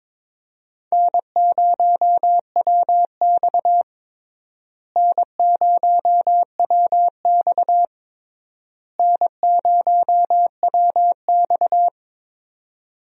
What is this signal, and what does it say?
Morse code, heard as a keyed tone, about 22 wpm, 700 Hz. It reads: N0WX N0WX N0WX
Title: USA Call Signs - 4 Characters - Repeated 3x - 22wpm